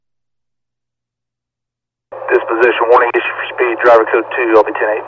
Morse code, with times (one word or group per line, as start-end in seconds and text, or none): none